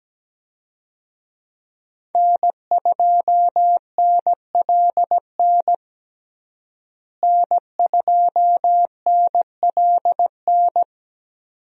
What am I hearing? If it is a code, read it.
Morse code, heard as a keyed tone, about 17 wpm, 700 Hz. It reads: N2NLN N2NLN
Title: USA Call Signs - 5 Characters - Repeated 2x - 17wpm